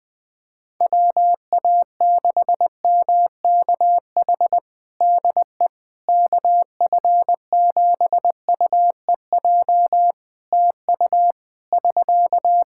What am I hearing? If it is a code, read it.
Morse code, heard as a keyed tone, about 20 wpm, 700 Hz. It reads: WA6MKH DE KF7UEJ TU <SK>
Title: K1USN SST QSO Practice 20wpm